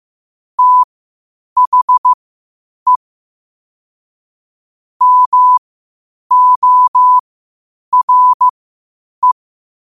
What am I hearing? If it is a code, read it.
Morse code, heard as a keyed tone, about 15 wpm, spaced as for 8 wpm, 1000 Hz. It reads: THE MORE
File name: qotdfiles / 8wpm